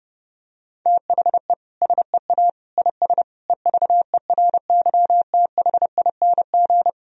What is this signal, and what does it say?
Morse code, heard as a keyed tone, about 32 wpm, 700 Hz. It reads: THE SEA IS EVERYTHING
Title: Sentences from Top 1000 Words - Review Part 1 - 30wpm